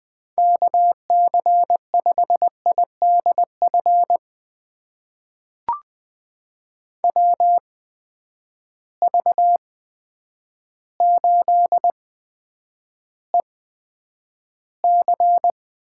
Morse code, heard as a keyed tone, about 21 wpm, spaced as for 9 wpm, 700 Hz. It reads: KC5IDF WV8EC
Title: USA Call Signs - 6 Characters - 8x Character Spacing - 20wpm